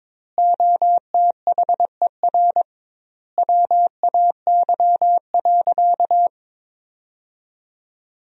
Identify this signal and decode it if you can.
Morse code, heard as a keyed tone, about 22 wpm, 700 Hz. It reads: OTHER WAY.
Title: 2x Word-Spacing - Sentences from Top 100 Words - 22wpm